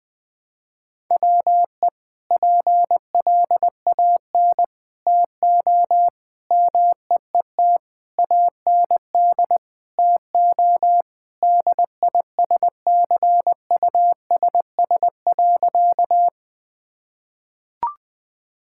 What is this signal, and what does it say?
Morse code, heard as a keyed tone, about 20 wpm, 700 Hz. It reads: WE PLAN TO MEET AND TO DISCUSS.
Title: Sentences with 2-Word Phrases - 20wpm